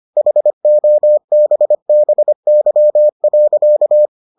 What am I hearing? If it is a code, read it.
Morse code, heard as a keyed tone, about 25 wpm, 600 Hz. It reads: HOBBY.